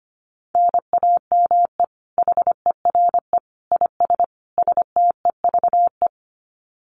Morse code, heard as a keyed tone, about 25 wpm, 700 Hz. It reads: NAME HERE IS STEVE
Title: Common QSO Phrases 25wpm